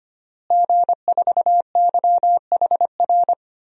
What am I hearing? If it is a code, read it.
Morse code, heard as a keyed tone, about 25 wpm, 700 Hz. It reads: G4YHR
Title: DXCC-100 Call Signs - 5 Characters - Repeated 3x - 25wpm